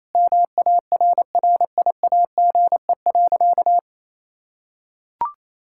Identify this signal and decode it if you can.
Morse code, heard as a keyed tone, about 27 wpm, 700 Hz. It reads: MARRIAGE.
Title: Sentences from Top 1000 Words - Review Part 6 - 28wpm